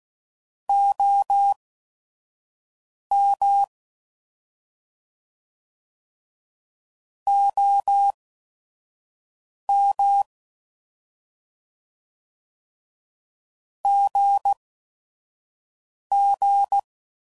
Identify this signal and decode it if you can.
Morse code, heard as a keyed tone, about 16 wpm, spaced as for 5 wpm, 800 Hz. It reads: OM OM GG
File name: Mixed / Abreviations / Abreviations